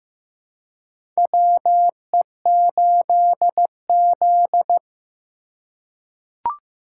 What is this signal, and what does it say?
Morse code, heard as a keyed tone, about 15 wpm, 700 Hz. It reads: WE8Z
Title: USA Call Signs - 4x Character Spacing - 15wpm